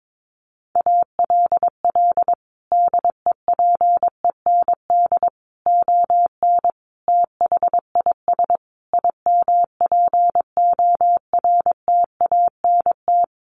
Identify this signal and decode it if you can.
Morse code, heard as a keyed tone, about 22 wpm, 700 Hz. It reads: ALL DEPEND ON THIS IMPORTANT
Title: Sentences from Top 1300 Words 22wpm